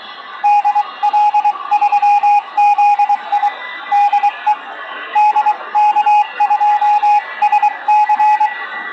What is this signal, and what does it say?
Morse code, heard as a keyed tone, about 23 wpm, 840 Hz, with radio noise. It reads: DL3ZI DE DK2SC